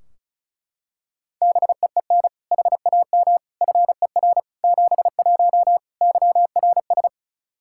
Morse code, heard as a keyed tone, about 34 wpm, 700 Hz. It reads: BEEN HAM FER 71 YRS
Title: Practice QSO - Exchange 3 - More Info - 35wpm